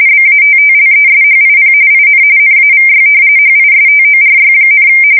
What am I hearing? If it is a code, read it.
RTTY, about 45.45 baud. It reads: BATTLESHIP IN NAVY HISTORY,
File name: fsk170 / battleshipnj